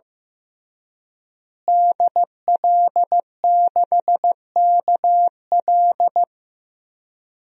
Morse code, heard as a keyed tone, about 15 wpm, 700 Hz. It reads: DL6KL
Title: DXCC-100 Call Signs - 5 Characters - Repeated 3x - 15wpm